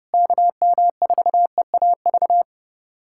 Morse code, heard as a keyed tone, about 30 wpm, 700 Hz. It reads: KM4EAV
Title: USA Callsign DE Callsign 30wpm